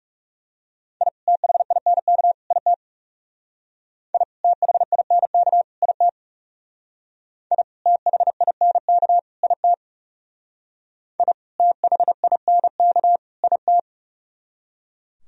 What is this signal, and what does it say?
Morse code, heard as a keyed tone, about 36 wpm, 700 Hz. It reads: I THINK IT I THINK IT I THINK IT I THINK IT
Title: Speed-Racing - Sets of 3 Words 30wpm